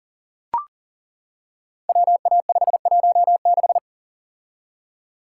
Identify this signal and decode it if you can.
Morse code, heard as a keyed tone, about 41 wpm, 700 Hz. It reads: WA516
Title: Code Groups - 5 Letter:Numbers - 40wpm